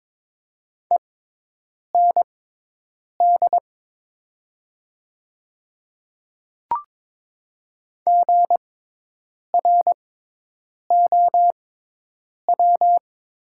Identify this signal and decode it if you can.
Morse code, heard as a keyed tone, about 22 wpm, spaced as for 8 wpm, 700 Hz. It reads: END GROW
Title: 6x Character Spacing - Top 300 Words - 22wpm